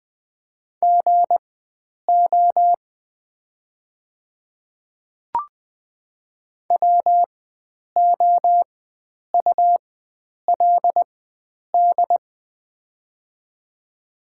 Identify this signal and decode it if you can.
Morse code, heard as a keyed tone, about 20 wpm, spaced as for 9 wpm, 700 Hz. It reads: GO WOULD
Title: 4x Character Spacing - Top 100 Words - 20wpm